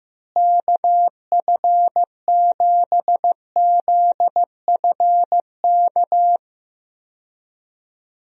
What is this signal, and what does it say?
Morse code, heard as a keyed tone, about 15 wpm, 700 Hz. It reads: KF7ZFK
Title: USA Call Signs - 6 Characters - Repeated 3x - 15wpm